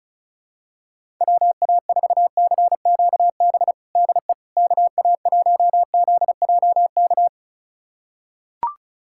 Morse code, heard as a keyed tone, about 34 wpm, 700 Hz. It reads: WA4CQB DE KA1ZJK
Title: USA Callsign DE Callsign - 6 Characters - 35wpm